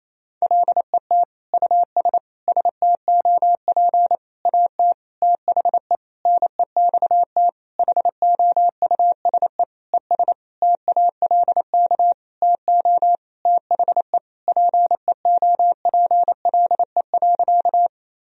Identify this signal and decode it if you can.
Morse code, heard as a keyed tone, about 28 wpm, 700 Hz. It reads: LET US STOP AT THE NEXT HOUSE ES TALK TO THE PEOPLE.
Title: Sentences from Top 400 Words and Abbreviations 28wpm